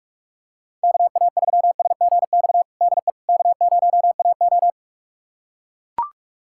Morse code, heard as a keyed tone, about 45 wpm, 690 Hz. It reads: KA3SGX DE K0AO
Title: USA Callsign DE Callsign 45wpm